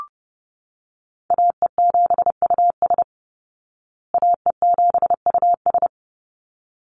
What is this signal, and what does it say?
Morse code, heard as a keyed tone, about 30 wpm, 700 Hz. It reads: AE7US AE7US
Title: Call Signs Repeated 3x - 30wpm